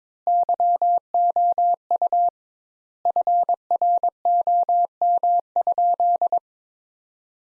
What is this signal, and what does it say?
Morse code, heard as a keyed tone, about 22 wpm, 700 Hz. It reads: YOU FROM?
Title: Sentences Top 200 Words - 22wpm